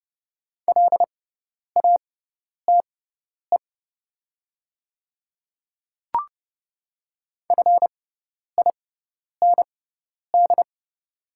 Morse code, heard as a keyed tone, about 30 wpm, spaced as for 10 wpm, 700 Hz. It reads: LATE FIND
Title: 6x Character Spacing - Top 300 Words - 30wpm